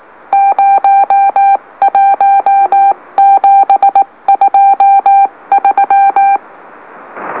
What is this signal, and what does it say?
Morse code, heard as a keyed tone, about 18 wpm, 790 Hz. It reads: 01723
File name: M04- -2345utc-LOLO